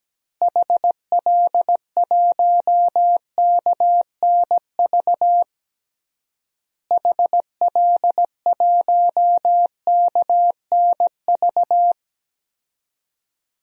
Morse code, heard as a keyed tone, about 17 wpm, 700 Hz. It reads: HL1KNV HL1KNV
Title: DXCC-100 Call Signs - 6 Characters - Repeated 3x - 17wpm